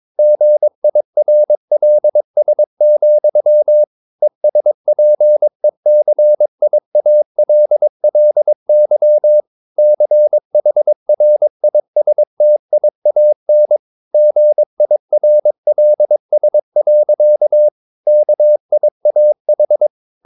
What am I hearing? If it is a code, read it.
Morse code, heard as a keyed tone, about 22 wpm, 600 Hz. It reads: GIRLS, ESPECIALLY CHRISTIAN GIRLS. KIAH
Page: Romanen Never af Ken Follet på engelsk.